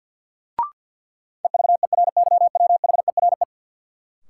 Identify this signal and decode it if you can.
Morse code, heard as a keyed tone, about 52 wpm, 700 Hz. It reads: EVERYWHERE
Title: Sentences from Top 1600 Words - New Word Drill Alt - 50wpm